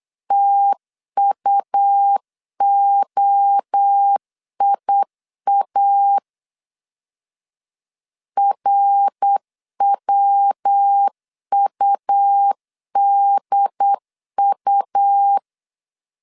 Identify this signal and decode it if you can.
Morse code, heard as a keyed tone, about 8 wpm, 800 Hz. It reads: TUOIA RWUDU
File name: CW / DET / 1 3 1500C S B 8wpm 4141